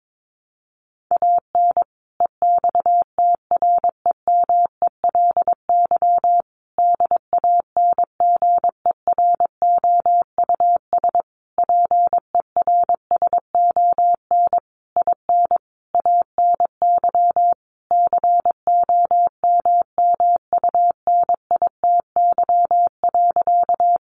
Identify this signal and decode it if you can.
Morse code, heard as a keyed tone, about 22 wpm, 700 Hz. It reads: AN EXTREMELY DANGEROUS PERSON IN ANY COMMUNITY.
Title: Sentences from Top 1200 Words 22wpm